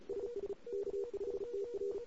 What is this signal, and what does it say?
Morse code, heard as a keyed tone, about 34 wpm, 400 Hz. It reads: SS K5TR
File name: CW / 2008-ss-cw-k5tr-op-n5rz -21:00